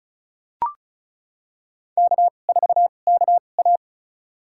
Morse code, heard as a keyed tone, about 34 wpm, spaced as for 25 wpm, 700 Hz. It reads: K4KA
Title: USA Call Signs - 4 Characters - 2x Character Spacing - 35wpm